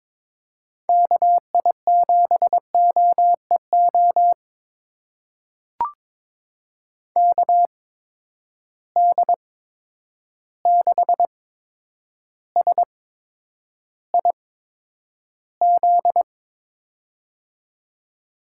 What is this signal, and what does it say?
Morse code, heard as a keyed tone, about 22 wpm, spaced as for 9 wpm, 700 Hz. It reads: KI7OEO KD6SIZ
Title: USA Call Signs - 6 Characters - 8x Character Spacing - 22wpm